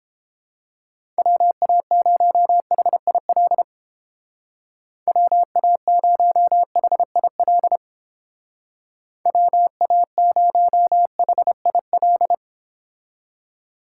Morse code, heard as a keyed tone, about 29 wpm, 700 Hz. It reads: WA0HIL WA0HIL WA0HIL